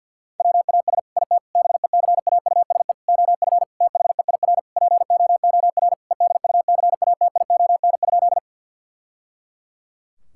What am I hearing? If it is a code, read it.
Morse code, heard as a keyed tone, about 50 wpm, 700 Hz. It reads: WAS IT BECAUSE OF THEIR POOR EDUCATION?
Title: Sentences from Top 1000 Words - Review Part 6 - 50wpm